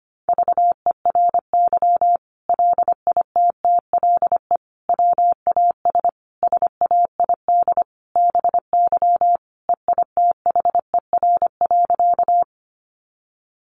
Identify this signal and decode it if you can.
Morse code, heard as a keyed tone, about 25 wpm, 700 Hz. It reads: VERY LITTLE WAS SAID BY EITHER.
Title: Sentences from Top 600 Words 25wpm